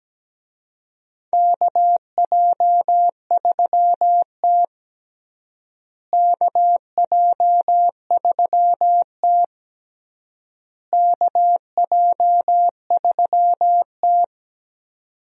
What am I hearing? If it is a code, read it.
Morse code, heard as a keyed tone, about 17 wpm, 700 Hz. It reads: KJ3T KJ3T KJ3T
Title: USA Call Signs - 4 Characters - Repeated 3x - 17wpm